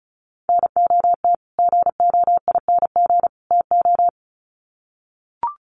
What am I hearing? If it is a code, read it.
Morse code, heard as a keyed tone, about 34 wpm, 700 Hz. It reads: NOT GOING TO
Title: Sets of 3 Words 35wpm